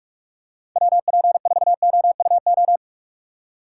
Morse code, heard as a keyed tone, about 47 wpm, 700 Hz. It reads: WP4OUO